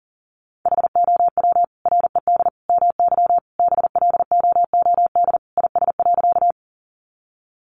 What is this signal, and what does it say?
Morse code, heard as a keyed tone, about 40 wpm, 700 Hz. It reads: HOW RED MY BLOOD IS.
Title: Sentences from Top 1100 Words 40wpm